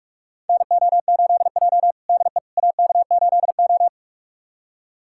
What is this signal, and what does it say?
Morse code, heard as a keyed tone, about 46 wpm, 700 Hz. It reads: NO8J DE AK8O
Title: USA Callsign DE Callsign - 4 Characters - 45wpm